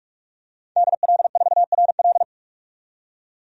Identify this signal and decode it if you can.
Morse code, heard as a keyed tone, about 45 wpm, 700 Hz. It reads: DL4RL
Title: DXCC-100 Call Signs - 5 Characters - 45wpm